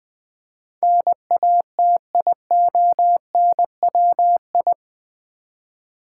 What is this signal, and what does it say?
Morse code, heard as a keyed tone, about 20 wpm, 700 Hz. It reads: NATIONWI
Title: Build A Word - 4 Letters At A Time - 10-Letter Words - 20wpm